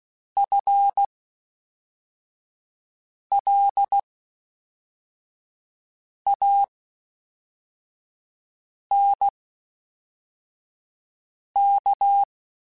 Morse code, heard as a keyed tone, about 16 wpm, spaced as for 4 wpm, 800 Hz. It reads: FLANK
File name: Spaced / FL Runs sp mp3s / FL Runs sp @5